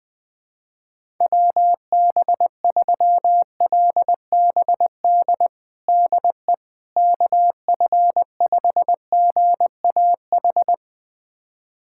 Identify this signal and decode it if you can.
Morse code, heard as a keyed tone, about 20 wpm, 700 Hz. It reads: WB3LBD DE KF5GAH
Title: USA Callsign DE Callsign 20wpm